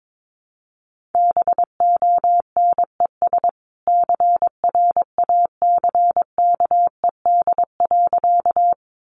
Morse code, heard as a keyed tone, about 22 wpm, 700 Hz. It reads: BONES CRACKED.
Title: A Sound Of Thunder 22wpm